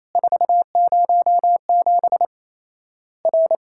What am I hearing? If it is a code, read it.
Morse code, heard as a keyed tone, about 28 wpm, 700 Hz. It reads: 407 R
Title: Worked All Europe DX Contest - QTC Exchange - 28wpm